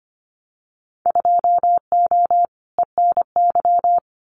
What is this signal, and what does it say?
Morse code, heard as a keyed tone, about 25 wpm, 700 Hz. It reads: 2O ENY
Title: Winter Field Day Exchanges 25wpm